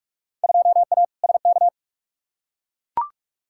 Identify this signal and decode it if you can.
Morse code, heard as a keyed tone, about 42 wpm, 700 Hz. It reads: 2A SK
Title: Field Day Exchanges Encore - 45wpm